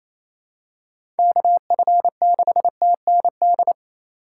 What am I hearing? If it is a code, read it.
Morse code, heard as a keyed tone, about 28 wpm, 700 Hz. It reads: KF6TND
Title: USA Call Signs - 6 Characters - 4x Character Spacing - 28wpm